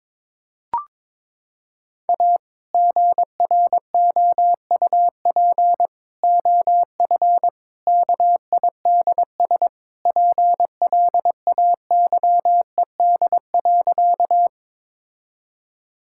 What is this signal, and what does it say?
Morse code, heard as a keyed tone, about 22 wpm, 700 Hz. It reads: A GROUP OF KIDS PLAYED.
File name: practice / 22wpm / Sentences with 3-Word Phrases - 22wpm